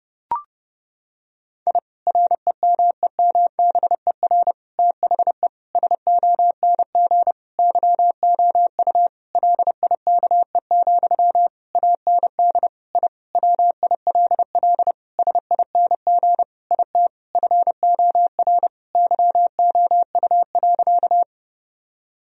Morse code, heard as a keyed tone, about 32 wpm, 700 Hz. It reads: I REMEMBER THE SONG YOU LIKE, AND I WILL SING IT FOR YOU.
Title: Sentences from Top 1000 Words - Review Part 2 - 30wpm